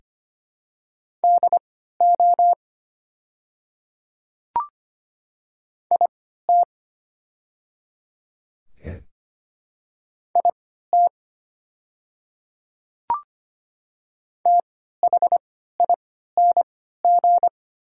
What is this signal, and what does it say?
Morse code, heard as a keyed tone, about 25 wpm, spaced as for 14 wpm, 690 Hz. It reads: DO IT IT THING
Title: 3x Character Spacing - Top 100 Words - 25wpm